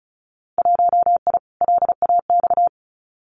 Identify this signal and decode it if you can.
Morse code, heard as a keyed tone, about 34 wpm, 700 Hz. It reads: 1I LAX
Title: Winter Field Day Exchanges Encore 35wpm